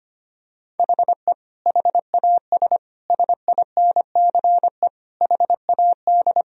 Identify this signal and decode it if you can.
Morse code, heard as a keyed tone, about 25 wpm, 700 Hz. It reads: HE HAS SINCE HAD
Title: Sentences from Top 1000 Words - Review Part 2 - 25wpm